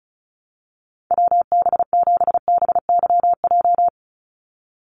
Morse code, heard as a keyed tone, about 34 wpm, 700 Hz. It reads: WB7BYJ
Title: USA Call Signs 35wpm